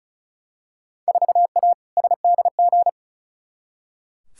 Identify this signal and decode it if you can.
Morse code, heard as a keyed tone, about 36 wpm, 700 Hz. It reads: 4A SDG
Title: Field Day Exchanges - 35wpm